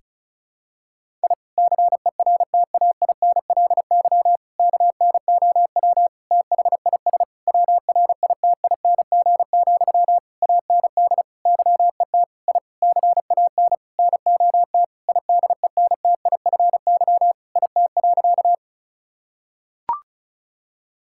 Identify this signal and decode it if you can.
Morse code, heard as a keyed tone, about 34 wpm, 700 Hz. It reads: I CERTAINLY KNOW THIS WRITING, AND YET I CAN NOT IDENTIFY IT.
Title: Sentences from Top 1000 Words - Review Part 6 - 35wpm